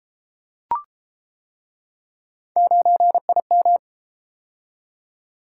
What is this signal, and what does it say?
Morse code, heard as a keyed tone, about 32 wpm, 700 Hz. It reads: 9IM